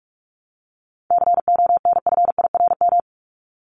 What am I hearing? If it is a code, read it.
Morse code, heard as a keyed tone, about 44 wpm, 700 Hz. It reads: CONFIRM
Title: Top 1751-2000 Words 45wpm